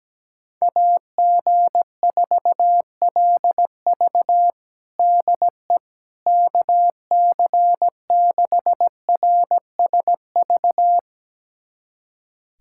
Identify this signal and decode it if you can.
Morse code, heard as a keyed tone, about 17 wpm, 700 Hz. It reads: AG4LV DE KC6RSV
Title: USA Callsign DE Callsign 17wpm